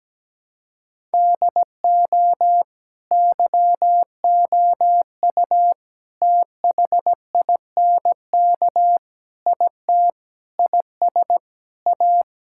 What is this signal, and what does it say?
Morse code, heard as a keyed tone, about 17 wpm, 700 Hz. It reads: DO YOU THINK IT IS A
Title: Sentences from Top 1000 Words - Review Part 6 - 17wpm